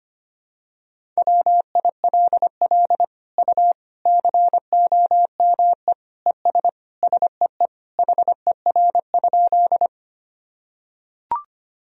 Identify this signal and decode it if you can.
Morse code, heard as a keyed tone, about 25 wpm, 700 Hz. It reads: WILL U COME ES SEE HER?
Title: Sentences from Top 100 Words and Abbreviations 25wpm